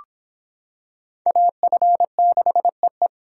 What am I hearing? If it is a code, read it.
Morse code, heard as a keyed tone, about 26 wpm, 700 Hz. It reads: AF6EE